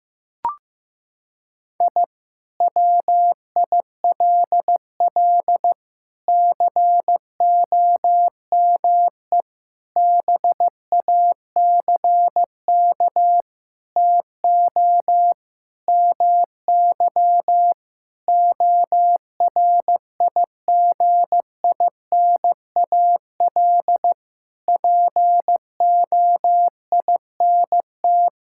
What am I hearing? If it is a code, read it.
Morse code, heard as a keyed tone, about 15 wpm, 700 Hz. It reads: I WILL COME BACK TO MY ORIGINAL POINT
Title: Sentences from Top 1000 Words - Review Part 6 - 15wpm